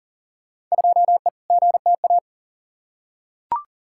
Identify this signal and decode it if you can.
Morse code, heard as a keyed tone, about 40 wpm, 700 Hz. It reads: 2E GTA
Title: Field Day Exchanges Encore - 40wpm